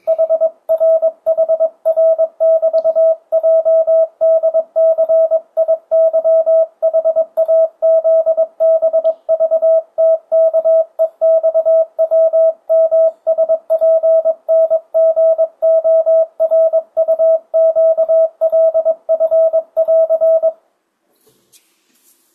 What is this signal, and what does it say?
Morse code, heard as a keyed tone, about 22 wpm, 650 Hz. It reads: HRHR<BT>JDCIYHAZBVTKEXWMSPNGORUQLF<AR>
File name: 2025cw / speedup / 22wpm